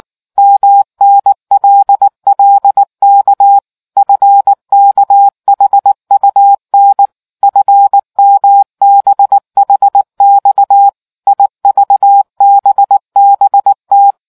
Morse code, heard as a keyed tone, about 19 wpm, 800 Hz. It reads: MNLLK FKHUN FMBHX IVBBT